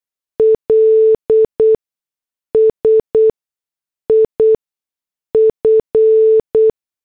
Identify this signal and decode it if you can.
Morse code, heard as a keyed tone, about 8 wpm, spaced as for 6 wpm, 440 Hz. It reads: LS IF